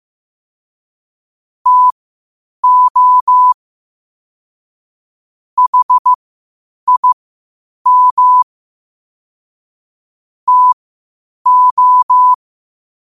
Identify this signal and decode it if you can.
Morse code, heard as a keyed tone, about 15 wpm, spaced as for 8 wpm, 1000 Hz. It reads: TO HIM TO
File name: qotdfiles / 8wpm